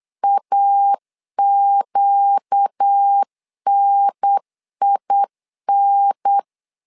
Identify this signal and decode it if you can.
Morse code, heard as a keyed tone, about 8 wpm, 800 Hz. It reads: AQNIN
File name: 4 Q 8wpm